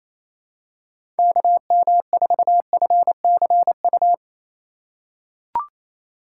Morse code, heard as a keyed tone, about 28 wpm, 700 Hz. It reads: KM4FCU
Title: USA Call Signs - 6 Characters - 3x Character Spacing - 28wpm